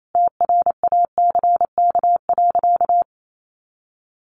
Morse code, heard as a keyed tone, about 28 wpm, 700 Hz. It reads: TRACK.
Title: Sentences from Top 1100 Words 28wpm